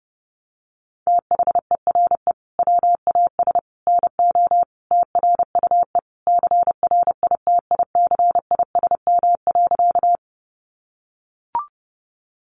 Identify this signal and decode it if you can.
Morse code, heard as a keyed tone, about 30 wpm, 700 Hz. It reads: THERE WAS NO TRUE CRITICISM.
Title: Sentences from Top 1300 Words 30wpm